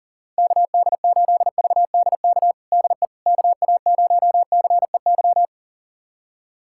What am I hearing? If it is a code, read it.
Morse code, heard as a keyed tone, about 37 wpm, 700 Hz. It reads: KD8VDK DE KA0CEY
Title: USA Callsign DE Callsign 40wpm